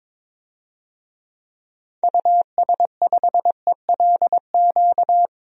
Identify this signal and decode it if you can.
Morse code, heard as a keyed tone, about 22 wpm, 700 Hz. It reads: US5ELQ
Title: DXCC-100 Call Signs - 6 Characters - Repeated 2x - 22wpm